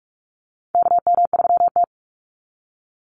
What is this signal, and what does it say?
Morse code, heard as a keyed tone, about 44 wpm, 700 Hz. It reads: KM3T
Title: Call Signs Repeated 2x - 45wpm